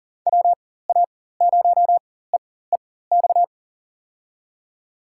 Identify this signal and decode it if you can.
Morse code, heard as a keyed tone, about 39 wpm, spaced as for 19 wpm, 700 Hz. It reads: WA0EEX
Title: USA Call Signs - 6 Characters - 4x Character Spacing - 40wpm